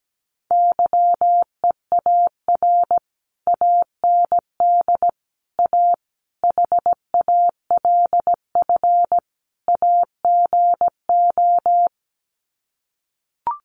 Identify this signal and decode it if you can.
Morse code, heard as a keyed tone, about 17 wpm, 700 Hz. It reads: YEAR AND A HALF AGO
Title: Sets of 5 Words Encore 17wpm